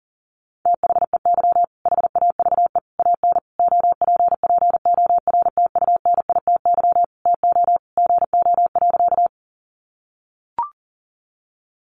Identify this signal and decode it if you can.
Morse code, heard as a keyed tone, about 40 wpm, 700 Hz. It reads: THEY HAVE AN OPPORTUNITY TO GO.
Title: Sentences from Top 1000 Words 40wpm